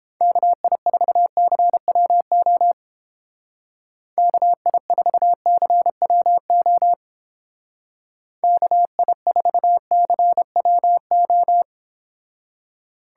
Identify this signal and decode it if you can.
Morse code, heard as a keyed tone, about 29 wpm, 700 Hz. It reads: KI4CWO KI4CWO KI4CWO